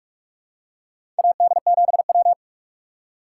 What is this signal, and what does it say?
Morse code, heard as a keyed tone, about 48 wpm, 700 Hz. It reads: AD7W
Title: USA Call Signs - 4 Characters - 2x Character Spacing - 45wpm